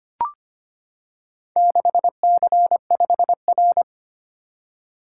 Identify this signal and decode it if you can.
Morse code, heard as a keyed tone, about 25 wpm, 700 Hz. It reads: 6C5R
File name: practice / 25wpm / Four Letter-Number Combinations - 25wpm